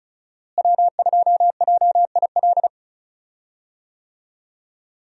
Morse code, heard as a keyed tone, about 36 wpm, 700 Hz. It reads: W2JIL
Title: USA Call Signs - 5 Characters - Repeated 3x - 35wpm